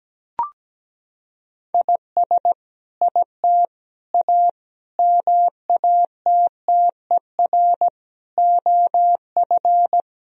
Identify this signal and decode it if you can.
Morse code, heard as a keyed tone, about 17 wpm, 700 Hz. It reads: IS IT A MATTER OF
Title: Sentences from Top 1000 Words - Review Part 3 - 17wpm